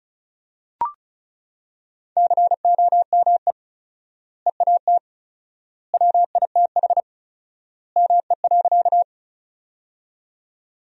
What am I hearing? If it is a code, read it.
Morse code, heard as a keyed tone, about 34 wpm, 700 Hz. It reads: COME EAT WITH ME.
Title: 4x Word-Spacing - Sentences from Top 300 Words - 35wpm